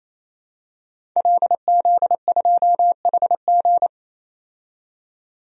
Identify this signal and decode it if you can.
Morse code, heard as a keyed tone, about 27 wpm, 700 Hz. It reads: LZ2HG
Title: DXCC-100 Call Signs - 5 Characters - 28wpm